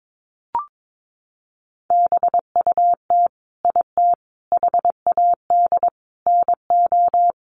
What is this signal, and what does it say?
Morse code, heard as a keyed tone, about 22 wpm, 700 Hz. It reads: BUT IT HAD NO
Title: Sentences from Top 1000 Words 22wpm